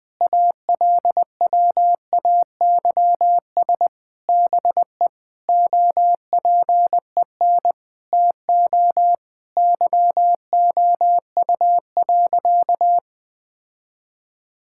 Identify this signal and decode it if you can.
Morse code, heard as a keyed tone, about 21 wpm, 700 Hz. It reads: ALWAYS BE OPEN TO YOU.
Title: Sentences from Top 1600 Words 20wpm